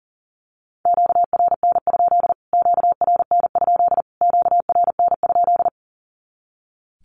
Morse code, heard as a keyed tone, about 40 wpm, 700 Hz. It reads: QRN? QRN? QRN?
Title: Q-Codes Repeated 3x 40wpm